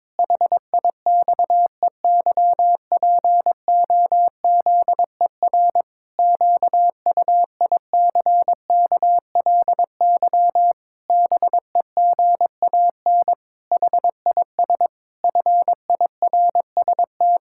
Morse code, heard as a keyed tone, about 22 wpm, 700 Hz. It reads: HIXEYPOZER QUICKLY BEGAN HIS FIRST
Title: Pangram Challenge v2 - 22wpm